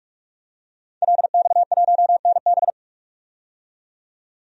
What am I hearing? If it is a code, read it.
Morse code, heard as a keyed tone, about 46 wpm, 700 Hz. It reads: LX1NB
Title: DXCC-100 Call Signs - 5 Characters - 45wpm